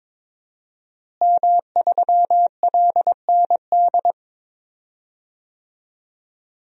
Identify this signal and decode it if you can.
Morse code, heard as a keyed tone, about 22 wpm, 700 Hz. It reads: M3LND
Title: DXCC-100 Call Signs - 5 Characters - Repeated 3x - 22wpm